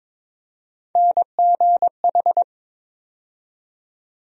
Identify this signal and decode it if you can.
Morse code, heard as a keyed tone, about 22 wpm, 700 Hz. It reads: NGH
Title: USA Call Sign Postfixes - 3 Letters 22wpm